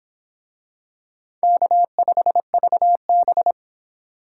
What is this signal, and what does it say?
Morse code, heard as a keyed tone, about 24 wpm, 700 Hz. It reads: K5VB